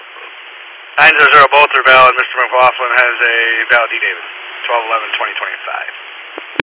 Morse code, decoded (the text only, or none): none